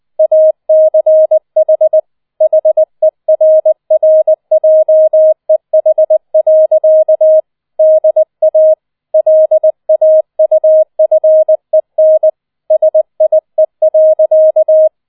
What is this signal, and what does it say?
Morse code, heard as a keyed tone, about 20 wpm, 630 Hz. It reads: ACH HERRJEH. DA LAUFEN SIE.